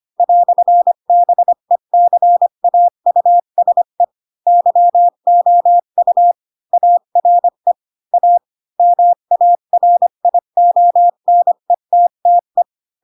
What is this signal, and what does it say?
Morse code, heard as a keyed tone, about 25 wpm, 700 Hz. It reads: "BECAUSE YOU ARE A MARIONETTE
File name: Pinocchio / Pinocchio ch4 20 25 ik1hge